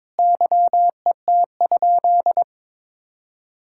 Morse code, heard as a keyed tone, about 22 wpm, 700 Hz. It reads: YET?
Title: Sentences from Top 1000 Words - Review Part 5 - 22wpm